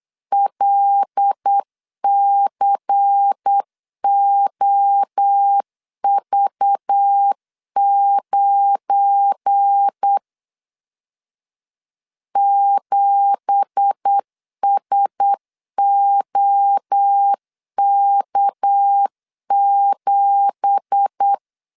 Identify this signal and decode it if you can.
Morse code, heard as a keyed tone, about 8 wpm, 800 Hz. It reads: LCOV9 7SOK7
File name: 8wpm